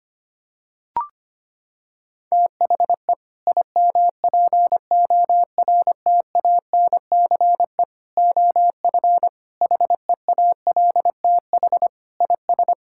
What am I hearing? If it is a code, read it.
Morse code, heard as a keyed tone, about 25 wpm, 700 Hz. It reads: THE IMPORTANCE OF HEALTH IS
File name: practice / 25wpm / Sentences with 3-Word Phrases - 25wpm